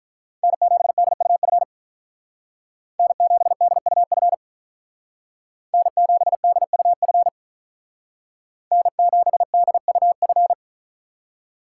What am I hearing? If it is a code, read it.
Morse code, heard as a keyed tone, about 43 wpm, 700 Hz. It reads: N7DUF N7DUF N7DUF N7DUF
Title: Speed Racing - USA 1x3 Call Signs 35wpm